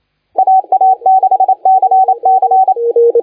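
Morse code, heard as a keyed tone, about 29 wpm, 760 Hz, with radio noise. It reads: AA6CC
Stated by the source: Morse contest audio recorded by LA4C during the SAC CW 2018 contest.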